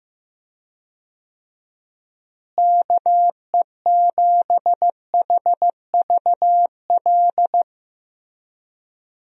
Morse code, heard as a keyed tone, about 15 wpm, 700 Hz. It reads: KE7HVL
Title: USA Call Signs - 6 Characters - Repeated 2x - 15wpm